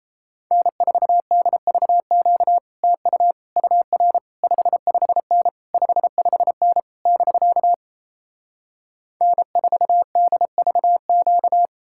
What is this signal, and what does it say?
Morse code, heard as a keyed tone, about 37 wpm, 700 Hz. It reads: N4DVQ TU UR 55N 55N <BK> N4DVQ